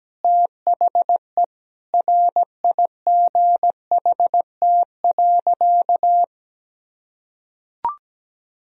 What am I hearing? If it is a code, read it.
Morse code, heard as a keyed tone, about 17 wpm, 700 Hz. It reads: THE RIGHT.
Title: Sentences from Top 1000 Words - Review Part 1 - 17wpm